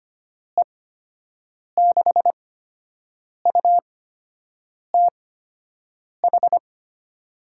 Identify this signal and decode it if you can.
Morse code, heard as a keyed tone, about 25 wpm, spaced as for 7 wpm, 700 Hz. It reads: E6UTH
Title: USA Call Signs - 6 Characters - 8x Character Spacing - 25wpm